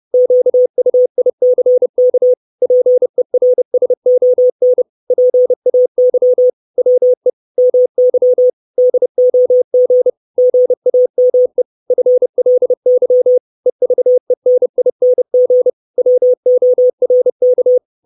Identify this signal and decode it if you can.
Morse code, heard as a keyed tone, about 30 wpm, 500 Hz. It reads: QUICK PERSON PAY WE MY DOG GAME FLY EVENING WORK
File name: 632 Words 30WPM